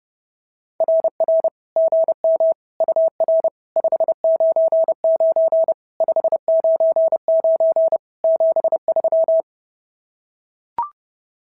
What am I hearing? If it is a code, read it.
Morse code, heard as a keyed tone, about 30 wpm, 650 Hz. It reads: RR GM UR 599 599 73
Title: SOTA QSO Practice 30wpm